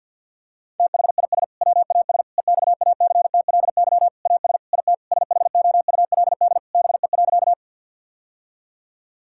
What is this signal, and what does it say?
Morse code, heard as a keyed tone, about 52 wpm, 700 Hz. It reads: THIS WAS EXACTLY AS IT SHOULD BE.
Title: Sentences from Top 1000 Words - Review Part 6 - 50wpm